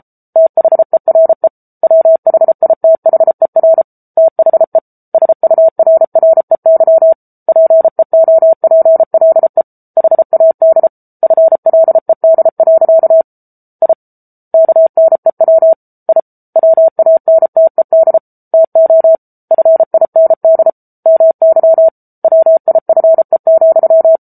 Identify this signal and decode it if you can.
Morse code, heard as a keyed tone, about 34 wpm, 650 Hz. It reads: THERE WHITHER THE SURREY PEOPLE HAD FLED. I KNEW I WANTED TO FIND MY WIFE,
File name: Book Two / 07-Chapter 7 - 33 WPM